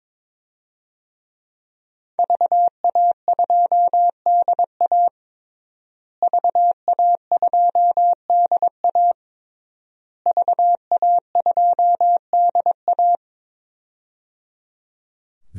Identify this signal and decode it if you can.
Morse code, heard as a keyed tone, about 21 wpm, 700 Hz. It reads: VA2DA VA2DA VA2DA
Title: DXCC-100 Call Signs - 5 Characters - Repeated 3x - 22wpm